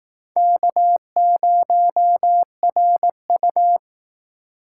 Morse code, heard as a keyed tone, about 18 wpm, 700 Hz. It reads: K0RU
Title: Speed Racing - USA 1x2 Call Signs 15wpm